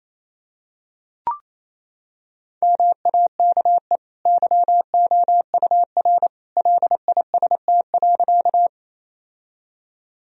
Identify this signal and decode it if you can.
Morse code, heard as a keyed tone, about 28 wpm, 700 Hz. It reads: MAKE YOUR LIST.
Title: Sentences from Top 1000 Words - Review Part 2 - 28wpm